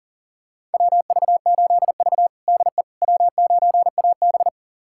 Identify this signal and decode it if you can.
Morse code, heard as a keyed tone, about 40 wpm, 700 Hz. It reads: WV8V DE W9AB
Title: USA Callsign DE Callsign - 4 Characters - 40wpm